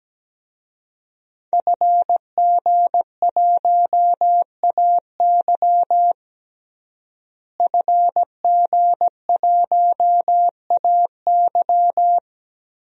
Morse code, heard as a keyed tone, about 17 wpm, 700 Hz. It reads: FG1AY FG1AY
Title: DXCC-100 Call Signs - 5 Characters - Repeated 3x - 17wpm